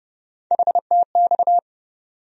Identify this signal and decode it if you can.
Morse code, heard as a keyed tone, about 30 wpm, 700 Hz. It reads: HTX
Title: USA Call Sign Postfixes - 3 Letters 30wpm